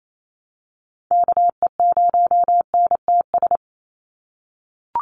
Morse code, heard as a keyed tone, about 28 wpm, 700 Hz. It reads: KE0NTS
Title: USA Call Signs 28wpm